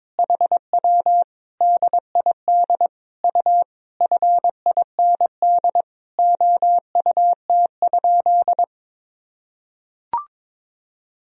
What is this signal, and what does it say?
Morse code, heard as a keyed tone, about 22 wpm, 700 Hz. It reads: HW DID U FIND OUT?
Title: Sentences from Top 200 Words and Abbreviations 22wpm